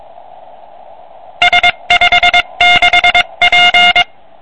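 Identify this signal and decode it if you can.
Morse code, heard as a keyed tone, about 22 wpm, 2150 Hz. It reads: S56P
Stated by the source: CW from a QRP QCX kit transceiver on 80 m, 5 W.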